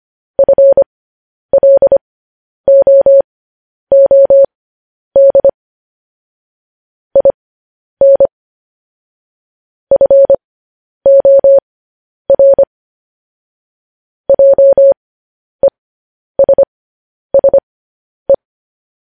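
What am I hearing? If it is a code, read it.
Morse code, heard as a keyed tone, about 25 wpm, spaced as for 10 wpm, 550 Hz. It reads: FLOOD IN FOR JESSE
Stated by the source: Morse code news headlines at 10.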